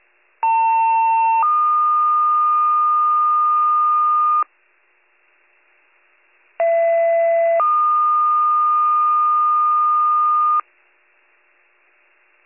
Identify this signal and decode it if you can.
Two-tone page, heard as a one-second tone, then a three-second tone, about 900 then 1220 Hz, about 670 then 1220 Hz. It is heard from 0.4 to 10.6 s.